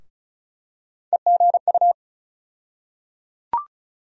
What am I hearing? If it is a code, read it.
Morse code, heard as a keyed tone, about 34 wpm, 700 Hz. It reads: EGU
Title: USA Call Sign Postfixes - 3 Letters 35wpm